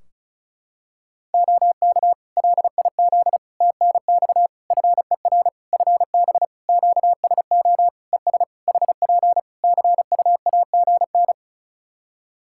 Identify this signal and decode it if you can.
Morse code, heard as a keyed tone, about 36 wpm, 700 Hz. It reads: OK LIZ TNX FER FB QSO ES HP CUAGN
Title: Practice QSO - Exchange 4 - Ending - 35wpm